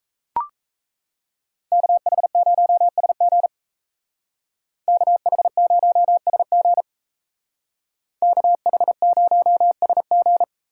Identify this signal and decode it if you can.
Morse code, heard as a keyed tone, about 36 wpm, 700 Hz. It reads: KH0SG KH0SG KH0SG